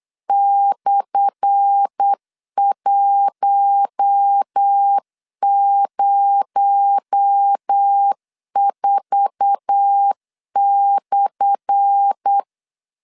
Morse code, heard as a keyed tone, about 8 wpm, 800 Hz. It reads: /104/
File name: LEZIONI / 10 SIGNS / 10A SLASH 8wpm 3452